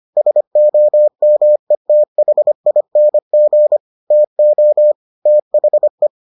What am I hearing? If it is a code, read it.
Morse code, heard as a keyed tone, about 25 wpm, 600 Hz. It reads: SOMETHING TO THE